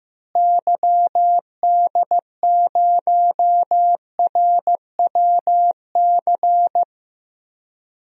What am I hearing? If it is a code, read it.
Morse code, heard as a keyed tone, about 15 wpm, 700 Hz. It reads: YD0RWC
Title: DXCC-100 Call Signs - 6 Characters - Repeated 2x - 15wpm